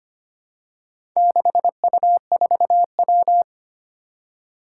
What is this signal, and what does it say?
Morse code, heard as a keyed tone, about 25 wpm, 700 Hz. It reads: 6U4W
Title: Four Challenging Letter-Number Combinations 25wpm